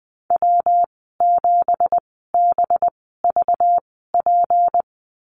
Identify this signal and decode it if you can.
Morse code, heard as a keyed tone, about 20 wpm, spaced as for 15 wpm, 700 Hz. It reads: W7BVP
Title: 213 Lesson 34g - Number 8 - Call Signs 20wpm